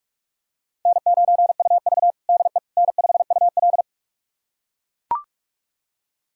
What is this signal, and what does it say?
Morse code, heard as a keyed tone, about 41 wpm, 700 Hz. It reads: N9UV DE N5UL
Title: USA Callsign DE Callsign - 4 Characters - 45wpm